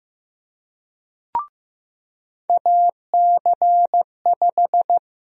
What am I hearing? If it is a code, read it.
Morse code, heard as a keyed tone, about 15 wpm, 700 Hz. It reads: AC5
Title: USA Call Sign Prefixes - Repeated 2x - 15wpm